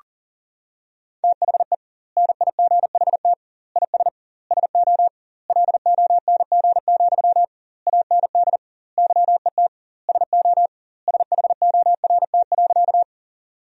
Morse code, heard as a keyed tone, about 37 wpm, 700 Hz. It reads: THE NIGHT IS SO LONG, AND YET SO SHORT.
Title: 2x Word-Spacing - Sentences from Top 500 Words - 40wpm